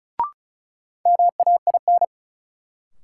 Morse code, heard as a keyed tone, about 36 wpm, 700 Hz. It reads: MAIN
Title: Sentences from Top 1000 Words - Review Part 2 - New Word Drill - 35wpm